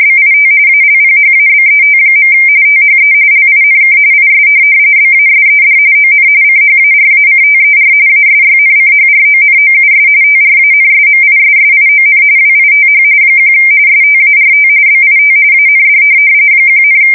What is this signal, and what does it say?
RTTY, about 45.45 baud. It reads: BIT ALPHABET. THE LINK IS BASED ON CHARACTER ASYNCHRONOUS TRANSMISSION WITH 1 START BIT AND 1,
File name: fsk170 / teletype-RTTY